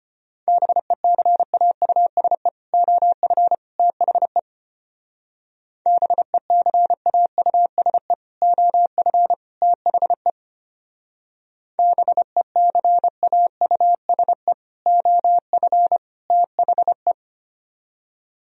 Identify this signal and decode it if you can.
Morse code, heard as a keyed tone, about 33 wpm, 700 Hz. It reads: BECAUSE OF THE BECAUSE OF THE BECAUSE OF THE